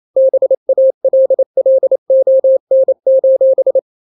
Morse code, heard as a keyed tone, about 28 wpm, 530 Hz. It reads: BALLON:
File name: daily / 27wpm